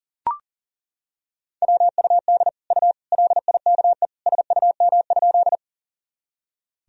Morse code, heard as a keyed tone, about 40 wpm, 700 Hz. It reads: WUD U LIKE SUM?
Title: Sentences from Top 100 Words and Abbreviations 40wpm